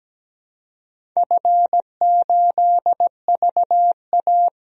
Morse code, heard as a keyed tone, about 17 wpm, 700 Hz. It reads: F8VA
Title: DXCC-100 Call Signs - 4 Characters - 17wpm